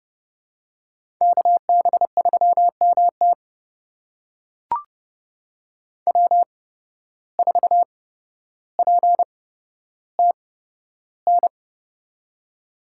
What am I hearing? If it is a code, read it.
Morse code, heard as a keyed tone, about 29 wpm, spaced as for 13 wpm, 700 Hz. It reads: KB3MT W4PTN
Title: USA Call Signs - 5 Characters - 8x Character Spacing - 30wpm